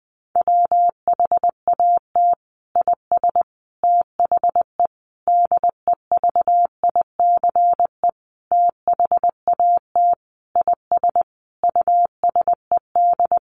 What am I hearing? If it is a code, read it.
Morse code, heard as a keyed tone, about 19 wpm, 700 Hz. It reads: WHAT IS THE DEVICE THAT IS USED
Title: Sentences from Top 1500 Words 20wpm